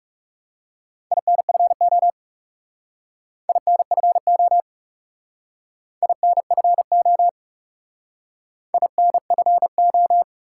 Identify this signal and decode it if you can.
Morse code, heard as a keyed tone, about 38 wpm, 700 Hz. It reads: INFO INFO INFO INFO